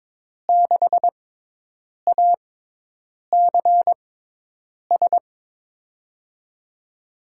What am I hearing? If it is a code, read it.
Morse code, heard as a keyed tone, about 22 wpm, spaced as for 8 wpm, 700 Hz. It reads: 6ACS
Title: USA Call Signs - 6x Character Spacing - 22wpm